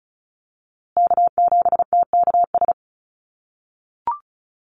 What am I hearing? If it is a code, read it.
Morse code, heard as a keyed tone, about 34 wpm, 700 Hz. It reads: K7TKS
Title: USA Call Signs Repeated 3x 35wpm